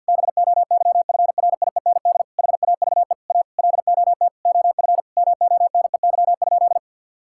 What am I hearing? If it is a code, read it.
Morse code, heard as a keyed tone, about 50 wpm, 700 Hz. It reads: BOYFRIEND HAVE A LOT OF MONEY?
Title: Sentences from Top 1000 Words - Review Part 5 - 50wpm